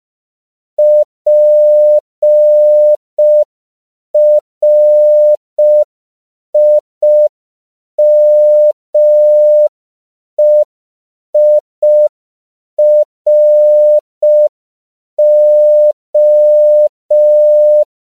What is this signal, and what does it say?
Morse code, heard as a keyed tone, about 5 wpm, 600 Hz. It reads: PRIMEIRO
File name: CW / 5PPM